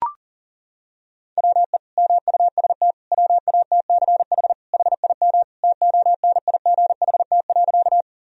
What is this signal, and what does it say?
Morse code, heard as a keyed tone, about 40 wpm, 700 Hz. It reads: WE MUST WATCH HIM TONIGHT.
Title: Sentences from Top 1000 Words - Review Part 3 - 40wpm